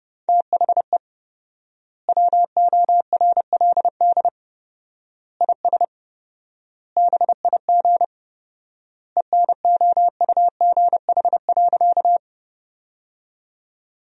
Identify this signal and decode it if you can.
Morse code, heard as a keyed tone, about 30 wpm, 700 Hz. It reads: THE WORLD IS BIG ENOUGH.
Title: 4x Word-Spacing - Sentences from Top 400 Words - 30wpm